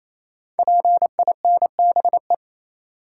Morse code, heard as a keyed tone, about 27 wpm, 700 Hz. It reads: PINBE
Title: Code Groups - 5 Letter:Numbers - 28wpm